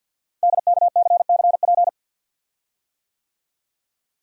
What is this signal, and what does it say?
Morse code, heard as a keyed tone, about 48 wpm, 700 Hz. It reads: DKCCP
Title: Code Groups - 5 Letters - 50wpm